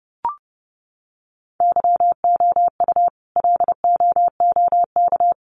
Morse code, heard as a keyed tone, about 30 wpm, 700 Hz. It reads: YOU LOOK
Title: Sentences from Top 1500 Words 30wpm